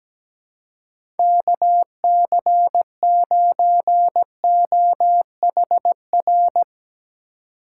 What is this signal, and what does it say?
Morse code, heard as a keyed tone, about 17 wpm, 700 Hz. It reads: KC9OHR
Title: USA Call Signs - 4x Character Spacing - 17wpm